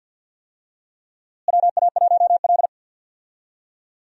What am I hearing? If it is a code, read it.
Morse code, heard as a keyed tone, about 52 wpm, 700 Hz. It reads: WA1L
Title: USA Call Signs - 4 Characters - Repeated 3x - 50wpm